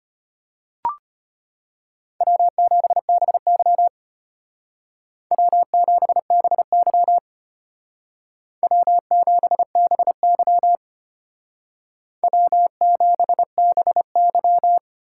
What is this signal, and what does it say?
Morse code, heard as a keyed tone, about 31 wpm, 700 Hz. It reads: W7BY W7BY W7BY W7BY
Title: Speed Racing - USA 1x2 Call Signs 25wpm